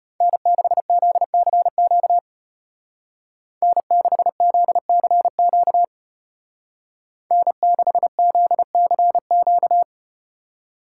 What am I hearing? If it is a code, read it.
Morse code, heard as a keyed tone, about 34 wpm, 700 Hz. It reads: N6ZCQ N6ZCQ N6ZCQ